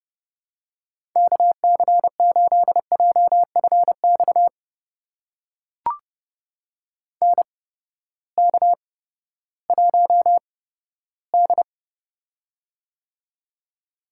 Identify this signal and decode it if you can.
Morse code, heard as a keyed tone, about 32 wpm, spaced as for 15 wpm, 700 Hz. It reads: KC8JFX NK1D
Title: USA Call Signs - 8x Character Spacing - 30wpm